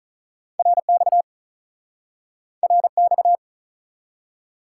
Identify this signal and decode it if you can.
Morse code, heard as a keyed tone, about 48 wpm, 700 Hz. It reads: RX RX